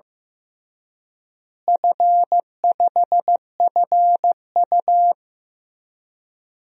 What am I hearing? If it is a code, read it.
Morse code, heard as a keyed tone, about 15 wpm, 700 Hz. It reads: F5FU
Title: DXCC-100 Call Signs - 4 Characters - Repeated 3x - 15wpm